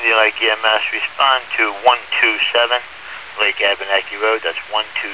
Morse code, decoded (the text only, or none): none